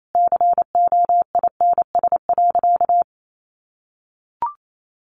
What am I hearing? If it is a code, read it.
Morse code, heard as a keyed tone, about 28 wpm, 700 Hz. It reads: COINS.
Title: A Sound Of Thunder 28wpm